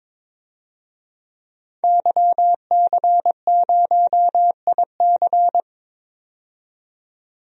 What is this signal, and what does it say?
Morse code, heard as a keyed tone, about 22 wpm, 700 Hz. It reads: YC0IC
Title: DXCC-100 Call Signs - 5 Characters - 22wpm